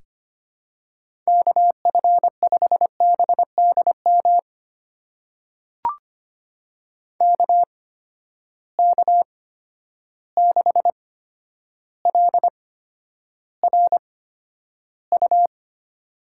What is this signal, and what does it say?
Morse code, heard as a keyed tone, about 25 wpm, spaced as for 11 wpm, 700 Hz. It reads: KF5BDM KK6LRU
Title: USA Call Signs - 6 Characters - 8x Character Spacing - 25wpm